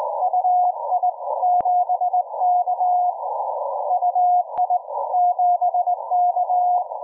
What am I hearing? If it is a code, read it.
Morse code, heard as a keyed tone, about 19 wpm, 720 Hz. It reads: UI7K UI7K
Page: 200мВт приличная мощность.